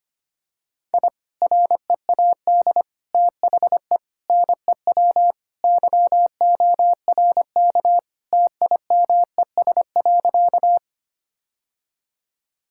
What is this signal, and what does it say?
Morse code, heard as a keyed tone, about 25 wpm, 700 Hz. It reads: I READ THE NEW YORK TIMES.
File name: practice / 25wpm / Sentences with 3-Word Phrases - 25wpm